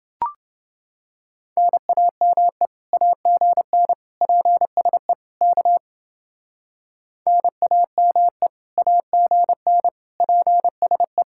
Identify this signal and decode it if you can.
Morse code, heard as a keyed tone, about 29 wpm, 700 Hz. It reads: NAME AGN PSE K NAME AGN PSE